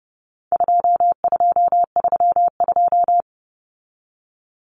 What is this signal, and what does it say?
Morse code, heard as a keyed tone, about 30 wpm, 700 Hz. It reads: 2232
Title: Common Cut Numbers - 4-Digit 30wpm